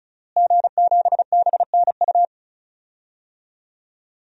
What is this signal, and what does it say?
Morse code, heard as a keyed tone, about 34 wpm, 700 Hz. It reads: G7BNU
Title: DXCC-100 Call Signs - 5 Characters - Repeated 3x - 35wpm